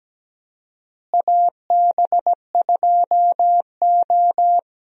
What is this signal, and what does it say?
Morse code, heard as a keyed tone, about 17 wpm, 700 Hz. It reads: AB2O
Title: USA Call Signs - 4 Characters - Repeated 3x - 17wpm